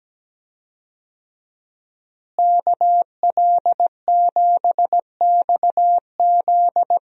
Morse code, heard as a keyed tone, about 17 wpm, 700 Hz. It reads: KL7XZ
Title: DXCC-100 Call Signs - 5 Characters - Repeated 3x - 17wpm